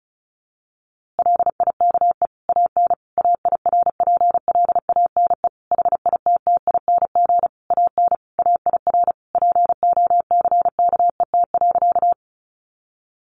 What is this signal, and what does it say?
Morse code, heard as a keyed tone, about 34 wpm, 700 Hz. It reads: LIKE AN AIRPLANE HITTING AN AIR POCKET.
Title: A Sound Of Thunder 35wpm